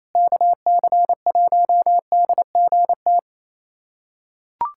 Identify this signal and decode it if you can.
Morse code, heard as a keyed tone, about 28 wpm, 700 Hz. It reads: KC1DGT
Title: USA Callsign DE Callsign - 6 Characters - 28wpm